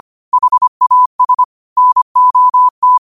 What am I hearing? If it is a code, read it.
Morse code, heard as a keyed tone, about 25 wpm, 1000 Hz. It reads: HAS NOT